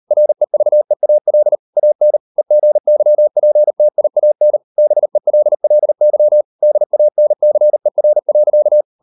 Morse code, heard as a keyed tone, about 38 wpm, 600 Hz. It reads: REVEAL AN EGYPTIAN BELLY DANCER.